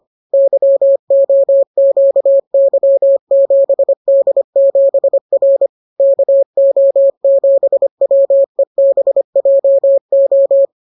Morse code, heard as a keyed tone, about 25 wpm, 550 Hz. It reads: YOQY7D7R KO7WEBJO